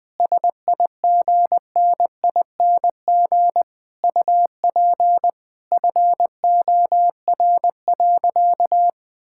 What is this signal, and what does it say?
Morse code, heard as a keyed tone, about 20 wpm, 700 Hz. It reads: SIGNING UP FOR.
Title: Sentences with 7-Word Phrases - 20wpm